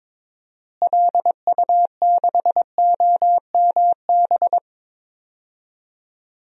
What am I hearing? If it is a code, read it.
Morse code, heard as a keyed tone, about 22 wpm, 700 Hz. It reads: LU6OMB
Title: DXCC-100 Call Signs - 6 Characters - Repeated 3x - 22wpm